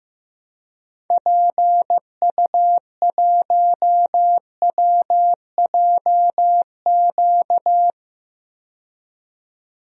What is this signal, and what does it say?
Morse code, heard as a keyed tone, about 15 wpm, 700 Hz. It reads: PU1WJQ
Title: DXCC-100 Call Signs - 6 Characters - 15wpm